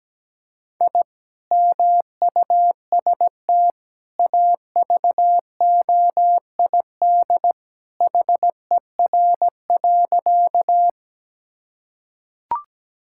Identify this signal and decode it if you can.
Morse code, heard as a keyed tone, about 17 wpm, 700 Hz. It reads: I MUST AVOID HER.
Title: Sentences from Top 1000 Words - Review Part 6 - 17wpm